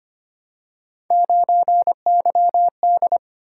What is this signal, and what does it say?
Morse code, heard as a keyed tone, about 25 wpm, 700 Hz. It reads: 9YD
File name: Three Letter-Number Combinations - 25wpm